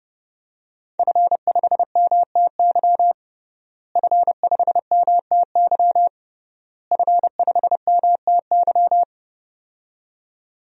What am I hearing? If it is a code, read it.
Morse code, heard as a keyed tone, about 30 wpm, 700 Hz. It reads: F5MTY F5MTY F5MTY
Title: DXCC-100 Call Signs - 5 Characters - Repeated 3x - 30wpm